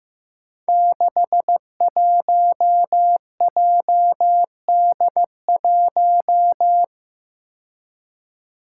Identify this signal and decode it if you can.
Morse code, heard as a keyed tone, about 15 wpm, 700 Hz. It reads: 61JD1
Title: Five Challenging Letter-Number Combinations 15wpm